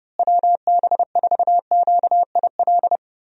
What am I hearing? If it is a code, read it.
Morse code, heard as a keyed tone, about 30 wpm, 700 Hz. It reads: WB4QIL
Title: USA Call Signs - 6 Characters - Repeated 3x - 30wpm